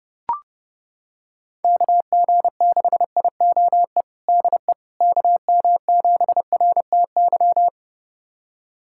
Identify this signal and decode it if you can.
Morse code, heard as a keyed tone, about 32 wpm, 700 Hz. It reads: KG6IOE DE KM7RTY
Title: USA Callsign DE Callsign 30wpm